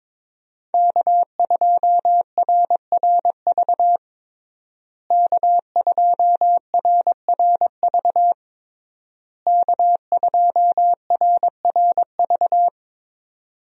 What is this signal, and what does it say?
Morse code, heard as a keyed tone, about 22 wpm, 700 Hz. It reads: K2RRV K2RRV K2RRV
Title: USA Call Signs - 5 Characters - Repeated 3x - 22wpm